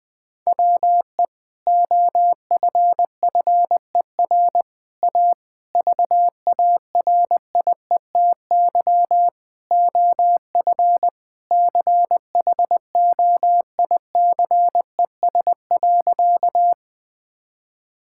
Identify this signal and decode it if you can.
Morse code, heard as a keyed tone, about 20 wpm, 700 Hz. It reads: WE OFFER A VARIETY OF CHOICES.
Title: Sentences with 3-Word Phrases - 20wpm